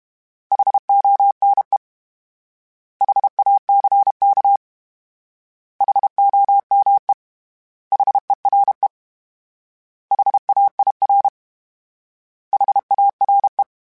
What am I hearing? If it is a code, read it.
Morse code, heard as a keyed tone, about 32 wpm, 800 Hz. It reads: HONE HACK HOME HERE HAIR HARE
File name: H Prefixes / H mp3s